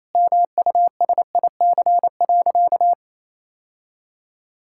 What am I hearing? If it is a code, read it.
Morse code, heard as a keyed tone, about 28 wpm, 700 Hz. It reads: MUSIC.
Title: Sentences from Top 1600 Words 28wpm